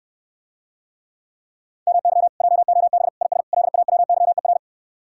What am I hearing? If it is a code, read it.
Morse code, heard as a keyed tone, about 69 wpm, 700 Hz. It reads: MY JOB IS LAWYER